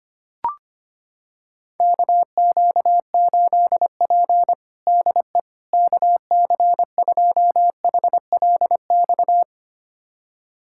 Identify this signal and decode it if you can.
Morse code, heard as a keyed tone, about 25 wpm, 700 Hz. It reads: KQ8P DE KC2HLX
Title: USA Callsign DE Callsign 25wpm